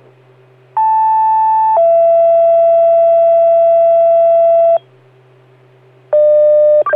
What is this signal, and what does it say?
Two-tone page, heard as a one-second tone, then a three-second tone, about 880 then 670 Hz. It is heard from 0.8 to 4.8 s.